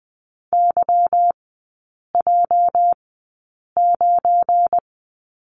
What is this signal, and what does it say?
Morse code, heard as a keyed tone, about 19 wpm, spaced as for 8 wpm, 700 Hz. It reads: YJ9
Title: 205 Lesson 33e - Letter j - Three Characters 20wpm